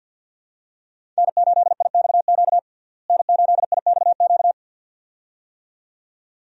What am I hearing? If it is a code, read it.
Morse code, heard as a keyed tone, about 50 wpm, 700 Hz. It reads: N8IXQ N8IXQ
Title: USA Call Signs - 5 Characters - Repeated 2x - 50wpm